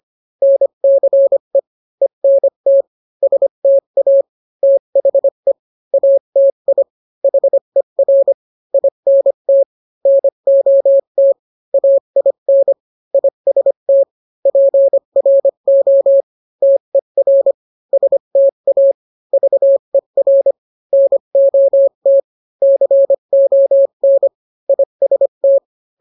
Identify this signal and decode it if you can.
Morse code, heard as a keyed tone, about 25 wpm, 550 Hz. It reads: NCE ENT STA THE ATI HER INT NOT AIN IST PRO TER STA VER NOT CON IST